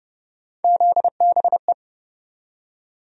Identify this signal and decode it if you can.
Morse code, heard as a keyed tone, about 29 wpm, 700 Hz. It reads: ZBE
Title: USA Call Sign Postfixes - 3 Letters 30wpm